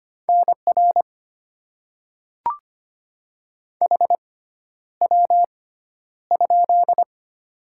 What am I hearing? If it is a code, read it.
Morse code, heard as a keyed tone, about 25 wpm, spaced as for 11 wpm, 690 Hz. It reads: NR HW?
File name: Top 90 QSO Elements - 6x Character Spacing - 25wpm